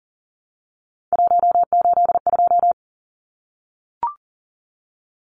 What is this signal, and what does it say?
Morse code, heard as a keyed tone, about 40 wpm, 700 Hz. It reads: 182
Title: Numbers - 3-Digit 40wpm